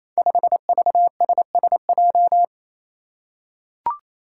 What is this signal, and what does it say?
Morse code, heard as a keyed tone, about 28 wpm, 700 Hz. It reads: 5VSSJ
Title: Five Challenging Letter-Number Combinations 28wpm